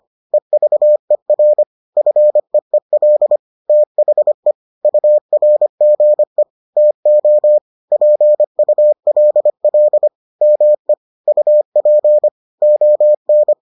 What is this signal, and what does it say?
Morse code, heard as a keyed tone, about 25 wpm, 600 Hz. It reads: EVER FEEL THE URGE TO PULL ME UP ON